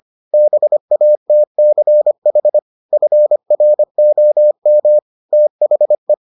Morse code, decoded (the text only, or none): BATCH FROM THE